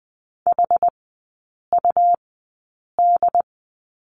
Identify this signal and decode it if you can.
Morse code, heard as a keyed tone, about 20 wpm, spaced as for 8 wpm, 700 Hz. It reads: HUD
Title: 133 Lesson 22e - Question Mark - Three Characters 20wpm